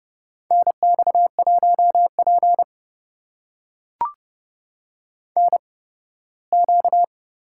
Morse code, heard as a keyed tone, about 30 wpm, spaced as for 18 wpm, 700 Hz. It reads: NX1P NQ
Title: USA Call Signs - 4 Characters - 8x Character Spacing - 30wpm